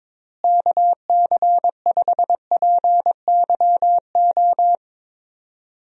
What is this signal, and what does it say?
Morse code, heard as a keyed tone, about 21 wpm, 700 Hz. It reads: KC5PYO
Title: USA Call Signs - 6 Characters - Repeated 3x - 22wpm